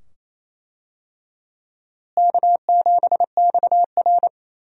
Morse code, heard as a keyed tone, about 28 wpm, 700 Hz. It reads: K7XR
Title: USA Call Signs - 4 Characters - Repeated 2x - 28wpm